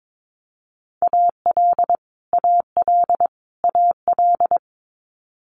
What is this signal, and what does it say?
Morse code, heard as a keyed tone, about 22 wpm, 700 Hz. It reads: AL AL AL
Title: Multi-letter Phonemes Repeated 3x 22wpm